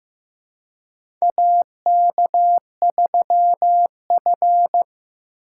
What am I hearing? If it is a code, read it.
Morse code, heard as a keyed tone, about 15 wpm, 700 Hz. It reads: AK3F
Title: USA Call Signs - 4 Characters - 2x Character Spacing - 15wpm